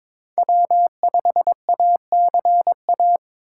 Morse code, heard as a keyed tone, about 22 wpm, 700 Hz. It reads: W5ACA
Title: USA Call Signs - 5 Characters - 3x Character Spacing - 22wpm